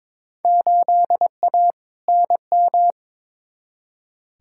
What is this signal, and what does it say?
Morse code, heard as a keyed tone, about 22 wpm, 700 Hz. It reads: 8A NM
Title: Field Day Exchanges Encore - 22wpm